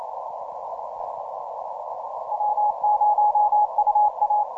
Morse code, heard as a keyed tone, about 29 wpm, 840 Hz. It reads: W0UA
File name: simple-file-list / VP6R 15MCW